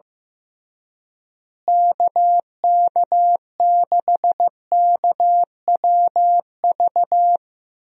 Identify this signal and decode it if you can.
Morse code, heard as a keyed tone, about 15 wpm, 700 Hz. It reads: KK6KWV
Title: USA Call Signs - 6 Characters - Repeated 3x - 15wpm